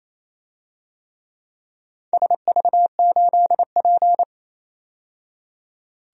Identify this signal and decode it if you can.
Morse code, heard as a keyed tone, about 28 wpm, 700 Hz. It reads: SV8P
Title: DXCC-100 Call Signs - 4 Characters - 28wpm